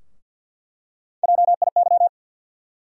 Morse code, heard as a keyed tone, about 53 wpm, 700 Hz. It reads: JIY
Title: USA Call Sign Postfixes - 3 Letters 50wpm